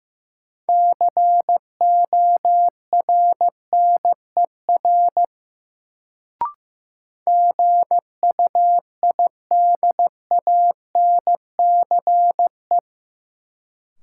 Morse code, heard as a keyed tone, about 15 wpm, 700 Hz. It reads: CORNER GUIDANCE
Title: Sentences from Top 1600 Words - New Word Drill - 15wpm